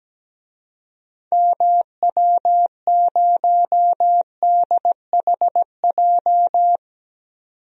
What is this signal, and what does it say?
Morse code, heard as a keyed tone, about 17 wpm, 700 Hz. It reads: MW0DHJ
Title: DXCC-100 Call Signs - 6 Characters - 17wpm